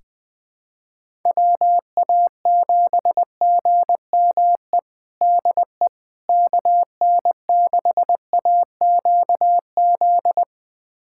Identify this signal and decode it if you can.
Morse code, heard as a keyed tone, about 21 wpm, 700 Hz. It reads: WA7GME DE KN6AQZ
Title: USA Callsign DE Callsign 20wpm